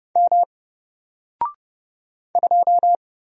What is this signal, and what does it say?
Morse code, heard as a keyed tone, about 31 wpm, 700 Hz. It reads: M 2
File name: Single-Letter-Number-30wpm